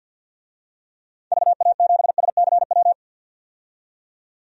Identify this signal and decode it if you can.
Morse code, heard as a keyed tone, about 50 wpm, 700 Hz. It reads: VA7SCW
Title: DXCC-100 Call Signs - 6 Characters - 50wpm